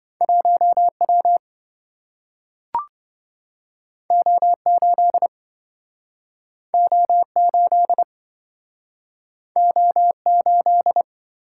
Morse code, heard as a keyed tone, about 27 wpm, 700 Hz. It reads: 1W O8 O8 O8